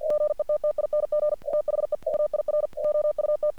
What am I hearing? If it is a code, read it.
Morse code, heard as a keyed tone, about 47 wpm, 600 Hz. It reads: GETTING THE AIR OUT
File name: archived / CRSnet